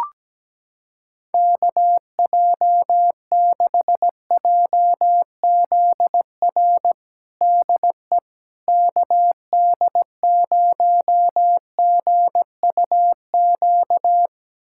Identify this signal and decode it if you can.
Morse code, heard as a keyed tone, about 17 wpm, 700 Hz. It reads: KJ6JZR DE KD0GUQ
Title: USA Callsign DE Callsign 17wpm